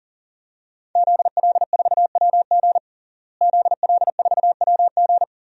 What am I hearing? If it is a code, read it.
Morse code, heard as a keyed tone, about 40 wpm, 700 Hz. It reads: ZL4WG ZL4WG
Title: DXCC-100 Call Signs - 5 Characters - Repeated 3x - 40wpm